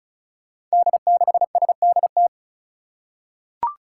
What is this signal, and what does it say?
Morse code, heard as a keyed tone, about 34 wpm, 700 Hz. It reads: D6SDT
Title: Five Challenging Letter-Number Combinations 35wpm